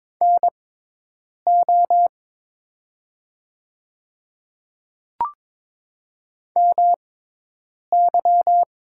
Morse code, heard as a keyed tone, about 22 wpm, spaced as for 8 wpm, 700 Hz. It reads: NO MY
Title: 6x Character Spacing - Top 200 Words - 22wpm